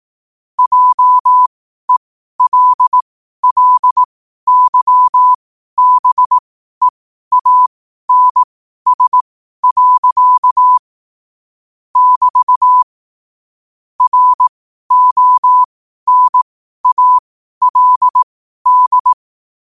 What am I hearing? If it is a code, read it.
Morse code, heard as a keyed tone, about 18 wpm, spaced as for 12 wpm, 1000 Hz. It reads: JELLYBEANS. <BT> RONALD
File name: qotdfiles / 12wpm